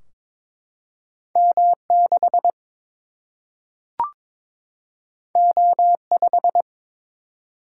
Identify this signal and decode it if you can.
Morse code, heard as a keyed tone, about 22 wpm, 700 Hz. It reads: M 6 O 5
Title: Single-Letter Single-Number 22wpm